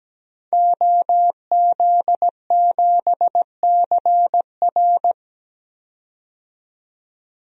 Morse code, heard as a keyed tone, about 17 wpm, 700 Hz. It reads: OZ7CR
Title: DXCC-100 Call Signs - 5 Characters - 17wpm